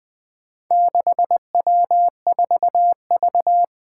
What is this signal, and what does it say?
Morse code, heard as a keyed tone, about 20 wpm, 700 Hz. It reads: 6W4V
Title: Four Challenging Letter-Number Combinations 20wpm